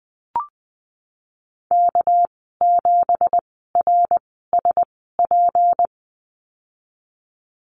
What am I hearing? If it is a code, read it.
Morse code, heard as a keyed tone, about 19 wpm, spaced as for 14 wpm, 700 Hz. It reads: K7RSP
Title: 238 Lesson 38g - Letter q - Call Signs 20wpm